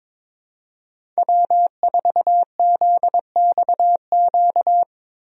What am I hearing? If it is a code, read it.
Morse code, heard as a keyed tone, about 21 wpm, 700 Hz. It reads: W4ZXQ
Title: USA Call Signs - 5 Characters - 22wpm